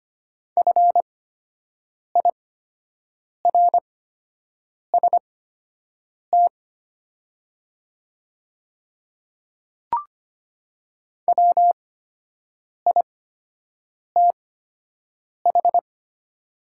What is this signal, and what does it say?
Morse code, heard as a keyed tone, about 25 wpm, spaced as for 7 wpm, 700 Hz. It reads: FIRST WITH
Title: 8x Character Spacing - Top 100 Words - 25wpm